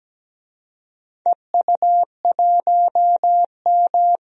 Morse code, heard as a keyed tone, about 17 wpm, 700 Hz. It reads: EU1M
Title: DXCC-100 Call Signs - 4 Characters - Repeated 2x - 17wpm